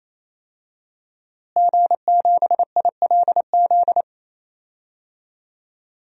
Morse code, heard as a keyed tone, about 28 wpm, 700 Hz. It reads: G7ILZ
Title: DXCC-100 Call Signs - 5 Characters - Repeated 3x - 28wpm